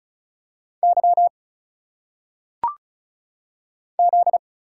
Morse code, heard as a keyed tone, about 36 wpm, 700 Hz. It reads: Y Z
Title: Single Letters - Phonetic Pronunciation 35wpm